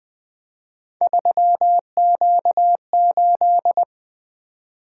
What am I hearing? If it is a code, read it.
Morse code, heard as a keyed tone, about 20 wpm, 700 Hz. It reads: 3Q8
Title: Three Letter-Number Combinations - 20wpm